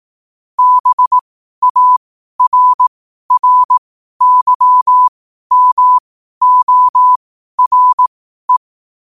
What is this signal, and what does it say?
Morse code, heard as a keyed tone, about 18 wpm, spaced as for 12 wpm, 1000 Hz. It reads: BARRYMORE
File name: qotdfiles / 12wpm